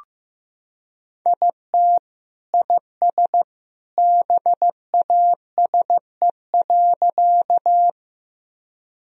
Morse code, heard as a keyed tone, about 15 wpm, 700 Hz. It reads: IT IS BASE.
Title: Sentences from Top 1000 Words - Review Part 1 - 15wpm